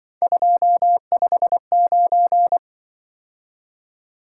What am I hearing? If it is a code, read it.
Morse code, heard as a keyed tone, about 24 wpm, 700 Hz. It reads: 259
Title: Speed-Racing - Three Letter-Number Combinations - 20wpm